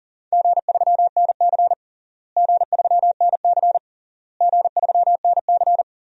Morse code, heard as a keyed tone, about 40 wpm, 700 Hz. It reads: G3NC G3NC G3NC
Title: DXCC-100 Call Signs - 4 Characters - Repeated 3x - 40wpm